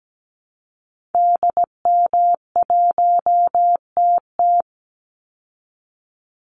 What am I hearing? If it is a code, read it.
Morse code, heard as a keyed tone, about 17 wpm, 700 Hz. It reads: DM1TT
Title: Call Signs Repeated 3x - 17wpm